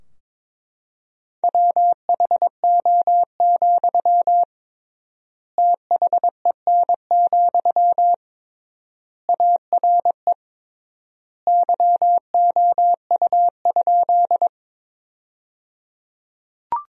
Morse code, heard as a keyed tone, about 22 wpm, 700 Hz. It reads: WHO, THEN, ARE YOU?
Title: 3x Word-Spacing - Sentences from Top 100 Words - 22wpm